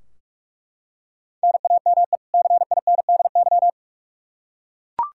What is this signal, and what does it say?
Morse code, heard as a keyed tone, about 41 wpm, 700 Hz. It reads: NAME CINDY
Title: K1USN SST QSO Practice 45wpm